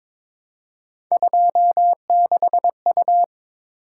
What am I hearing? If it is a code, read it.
Morse code, heard as a keyed tone, about 22 wpm, 700 Hz. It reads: 26U
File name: practice / 22wpm / Three Letter-Number Combinations - 22wpm